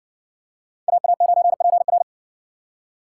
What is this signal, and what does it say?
Morse code, heard as a keyed tone, about 58 wpm, 700 Hz. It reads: AA9PR